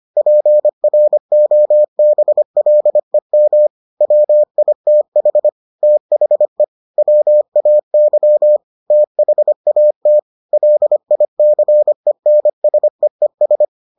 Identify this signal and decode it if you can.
Morse code, heard as a keyed tone, about 25 wpm, 600 Hz. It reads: PROBLEM WITH THE WAY THAT LICENSEES